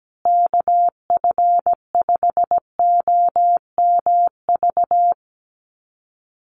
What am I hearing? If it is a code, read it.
Morse code, heard as a keyed tone, about 17 wpm, 700 Hz. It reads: KF5OMV
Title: USA Call Signs Repeated 2x 17wpm